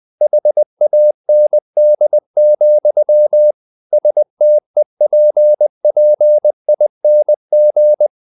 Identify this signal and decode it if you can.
Morse code, heard as a keyed tone, about 20 wpm, 600 Hz. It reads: HAND, STEPPING
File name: the old man and the sea part 2 20wpm